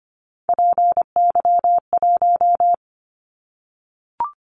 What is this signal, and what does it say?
Morse code, heard as a keyed tone, about 25 wpm, 700 Hz. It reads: PY1
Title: Callsign Prefixes for Top 100 DXCC - 25wpm